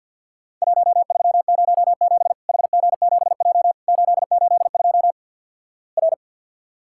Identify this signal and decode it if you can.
Morse code, heard as a keyed tone, about 50 wpm, 700 Hz. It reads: 1307 HG7J 882 R
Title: Worked All Europe DX Contest - QTC Exchange - 50wpm